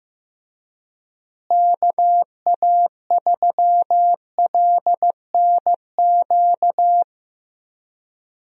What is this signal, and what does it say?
Morse code, heard as a keyed tone, about 15 wpm, 700 Hz. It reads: KA3LNQ
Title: USA Call Signs - 6 Characters - Repeated 2x - 15wpm